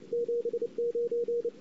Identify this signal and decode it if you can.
Morse code, heard as a keyed tone, about 29 wpm, 470 Hz, with radio noise. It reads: 79
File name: CW / 2008-ss-cw-k5tr-op-n5rz -23:00